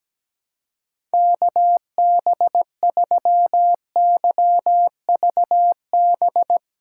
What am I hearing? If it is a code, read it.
Morse code, heard as a keyed tone, about 17 wpm, 700 Hz. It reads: KB3YVB
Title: USA Callsign DE Callsign - 6 Characters - 17wpm